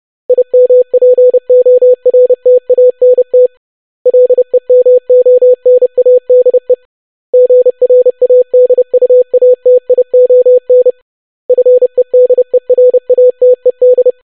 Morse code, heard as a keyed tone, about 30 wpm, 500 Hz. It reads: IMPORTANT LEMONADE GRADUATION FEDERATED
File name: longwords-25wpm